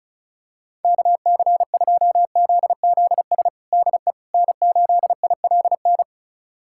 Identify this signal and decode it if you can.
Morse code, heard as a keyed tone, about 36 wpm, 700 Hz. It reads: KC2ZZS DE N8ILN
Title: USA Callsign DE Callsign 35wpm